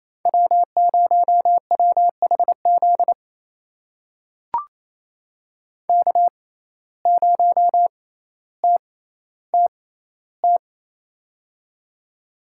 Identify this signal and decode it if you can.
Morse code, heard as a keyed tone, about 28 wpm, spaced as for 14 wpm, 700 Hz. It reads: W0WHZ K0TTT
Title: USA Call Signs - 5 Characters - 6x Character Spacing - 28wpm